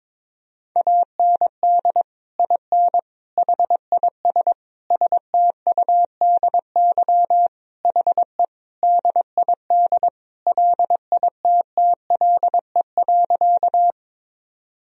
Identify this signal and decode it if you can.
Morse code, heard as a keyed tone, about 22 wpm, 700 Hz. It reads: AND IN HIS STUDY HE DID LITTLE.
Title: Sentences from Top 1000 Words - Review Part 1 - 22wpm